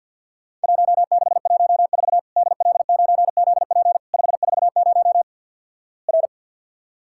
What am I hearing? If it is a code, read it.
Morse code, heard as a keyed tone, about 50 wpm, 700 Hz. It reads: 1614 DL9ZP 540 R
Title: Worked All Europe DX Contest - QTC Exchange with Repeats - 50wpm